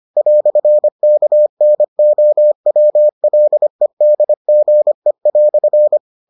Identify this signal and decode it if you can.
Morse code, heard as a keyed tone, about 25 wpm, 600 Hz. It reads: "KNOWLEDGE"